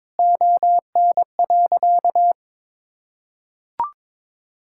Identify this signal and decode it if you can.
Morse code, heard as a keyed tone, about 22 wpm, 700 Hz. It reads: ON.
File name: Sentences with 9-Word Phrases - 22wpm